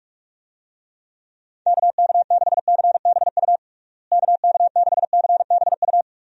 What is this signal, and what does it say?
Morse code, heard as a keyed tone, about 46 wpm, 700 Hz. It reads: KK6CBU KK6CBU
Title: USA Call Signs - 6 Characters - Repeated 3x - 45wpm